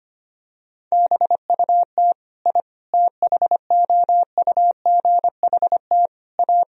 Morse code, heard as a keyed tone, about 25 wpm, 700 Hz. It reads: BUT I THOUGHT A
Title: Sentences from Top 1000 Words - Review Part 3 - 25wpm